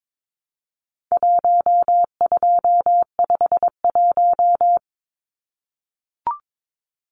Morse code, heard as a keyed tone, about 22 wpm, 700 Hz. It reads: 1251
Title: Numbers - Contest Style - 4-Digit 22wpm